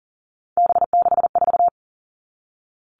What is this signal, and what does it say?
Morse code, heard as a keyed tone, about 40 wpm, 700 Hz. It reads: B64
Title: All Cut Numbers - 3-Digit 40wpm